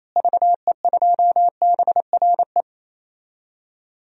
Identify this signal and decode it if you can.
Morse code, heard as a keyed tone, about 28 wpm, 700 Hz. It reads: VE2BRE
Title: DXCC-100 Call Signs - 6 Characters - Repeated 3x - 28wpm